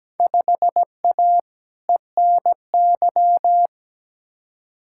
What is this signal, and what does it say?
Morse code, heard as a keyed tone, about 17 wpm, 700 Hz. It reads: 5A ENY
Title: Field Day Exchanges - 17wpm